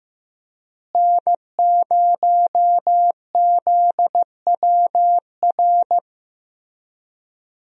Morse code, heard as a keyed tone, about 16 wpm, 700 Hz. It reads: N0ZWR
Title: USA Call Signs - 5 Characters - Repeated 3x - 15wpm